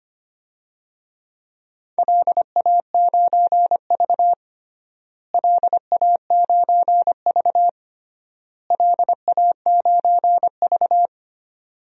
Morse code, heard as a keyed tone, about 25 wpm, 700 Hz. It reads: LA9V LA9V LA9V
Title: DXCC-100 Call Signs - 4 Characters - Repeated 3x - 25wpm